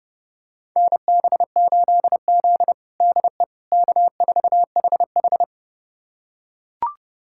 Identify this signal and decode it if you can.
Morse code, heard as a keyed tone, about 32 wpm, 700 Hz. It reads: NB8Z DE K4HH
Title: USA Callsign DE Callsign - 4 Characters - 30wpm